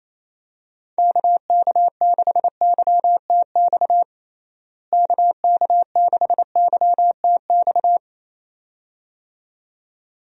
Morse code, heard as a keyed tone, about 28 wpm, 700 Hz. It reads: KK6YTX KK6YTX
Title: USA Call Signs - 6 Characters - Repeated 2x - 28wpm